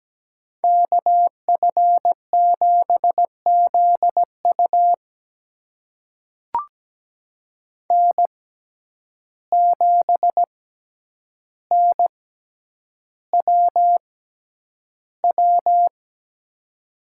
Morse code, heard as a keyed tone, about 17 wpm, spaced as for 9 wpm, 700 Hz. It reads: KF7ZU N7NWW
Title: USA Call Signs - 5 Characters - 6x Character Spacing - 17wpm